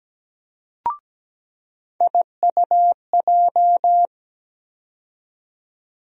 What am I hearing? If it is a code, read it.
Morse code, heard as a keyed tone, about 17 wpm, 700 Hz. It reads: IUJ
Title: USA Call Sign Postfixes - 3 Letters 17wpm